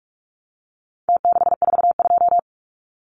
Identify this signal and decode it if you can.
Morse code, heard as a keyed tone, about 44 wpm, 700 Hz. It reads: T642
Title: Numbers - Contest Style - 4-Digit 45wpm